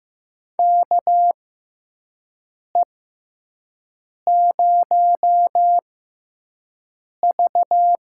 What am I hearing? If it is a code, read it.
Morse code, heard as a keyed tone, about 15 wpm, spaced as for 5 wpm, 700 Hz. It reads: KE0V
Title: USA Call Signs - 6x Character Spacing - 15wpm